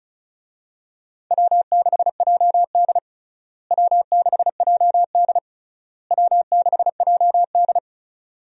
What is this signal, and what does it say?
Morse code, heard as a keyed tone, about 34 wpm, 700 Hz. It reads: W6JD W6JD W6JD
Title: USA Call Signs - 4 Characters - Repeated 3x - 35wpm